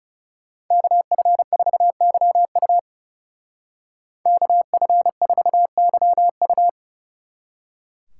Morse code, heard as a keyed tone, about 39 wpm, 700 Hz. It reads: KF4YU KF4YU